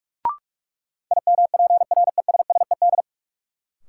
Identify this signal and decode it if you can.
Morse code, heard as a keyed tone, about 46 wpm, 700 Hz. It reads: IMPRESSED
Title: Sentences from Top 1600 Words - New Word Drill Alt - 45wpm